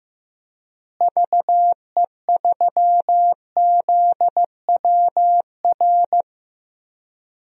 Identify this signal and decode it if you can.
Morse code, heard as a keyed tone, about 15 wpm, 700 Hz. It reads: VE3ZWR
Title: DXCC-100 Call Signs - 6 Characters - Repeated 3x - 15wpm